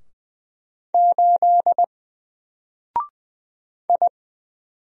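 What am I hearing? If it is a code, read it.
Morse code, heard as a keyed tone, about 20 wpm, 700 Hz. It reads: 8 I
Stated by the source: Single letter and number Morse drill at 20 wpm, from N4LCD.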